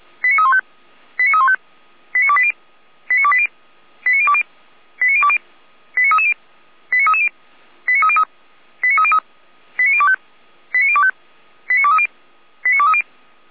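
Five-tone selective call (ZVEI), heard as quick runs of five tones, tones of about 70 ms. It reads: 89315 89315 89280 89280 89020 89020 89330 89330 89303 89303 89025 89025 89230 89230
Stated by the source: Five-tone ZVEI selective call received by SDR.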